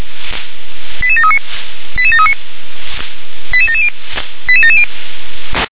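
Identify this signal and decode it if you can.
Five-tone selective call (ZVEI), heard as quick runs of five tones, tones of about 70 ms. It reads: 88739 88739 88770 88770
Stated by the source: Five-tone ZVEI selective call received with an SDR.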